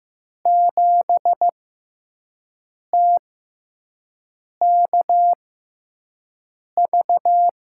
Morse code, heard as a keyed tone, about 15 wpm, spaced as for 5 wpm, 700 Hz. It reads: 7TKV
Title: USA Call Signs - 5 Characters - 6x Character Spacing - 15wpm